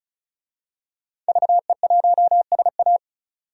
Morse code, heard as a keyed tone, about 34 wpm, 700 Hz. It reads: VE1SA
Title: DXCC-100 Call Signs - 5 Characters - 35wpm